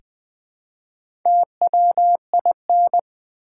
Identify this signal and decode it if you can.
Morse code, heard as a keyed tone, about 20 wpm, 700 Hz. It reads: TWIN
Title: Build A Word - 1 Letter At A Time - 4-Letter Words - 20wpm